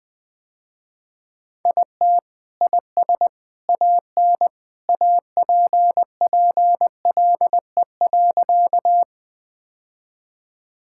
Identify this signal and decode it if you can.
Morse code, heard as a keyed tone, about 21 wpm, 700 Hz. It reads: IT IS AN APPLE.
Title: Sentences with 2-Word Phrases - Spoken Beginning - 20wpm